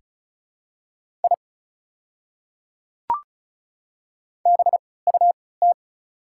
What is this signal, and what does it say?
Morse code, heard as a keyed tone, about 35 wpm, spaced as for 20 wpm, 700 Hz. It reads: I BUT
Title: 3x Character Spacing - Top 100 Words - 35wpm